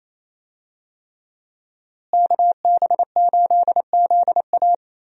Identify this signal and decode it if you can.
Morse code, heard as a keyed tone, about 28 wpm, 700 Hz. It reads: KB8ZA
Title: USA Call Signs - 5 Characters - Repeated 3x - 28wpm